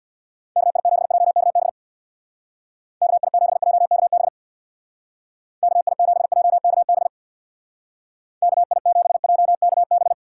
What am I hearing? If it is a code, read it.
Morse code, heard as a keyed tone, about 64 wpm, 700 Hz. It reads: KI7JKB KI7JKB KI7JKB KI7JKB